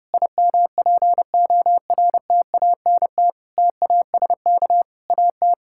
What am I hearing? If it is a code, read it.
Morse code, heard as a keyed tone, about 30 wpm, 700 Hz. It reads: IMPORTANT TASK AT
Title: Sentences from Top 1600 Words 30wpm